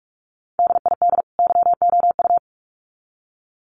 Morse code, heard as a keyed tone, about 44 wpm, 700 Hz. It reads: DID YOU
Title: Sets of 2 Words 45wpm